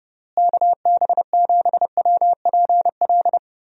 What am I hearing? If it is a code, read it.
Morse code, heard as a keyed tone, about 30 wpm, 700 Hz. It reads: KB7WPL
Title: USA Call Signs - 6 Characters - Repeated 2x - 30wpm